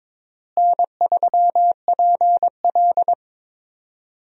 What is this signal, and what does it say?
Morse code, heard as a keyed tone, about 22 wpm, 700 Hz. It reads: N3PL
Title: USA Call Signs - 4 Characters - Repeated 3x - 22wpm